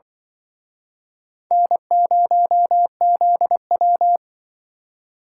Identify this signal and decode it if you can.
Morse code, heard as a keyed tone, about 24 wpm, 700 Hz. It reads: N0ZW
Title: Speed Racing - USA 1x2 Call Signs 20wpm